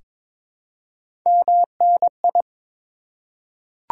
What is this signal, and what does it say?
Morse code, heard as a keyed tone, about 22 wpm, 690 Hz. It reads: MNI
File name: Top 90 QSO Elements - 4x Character Spacing - 22wpm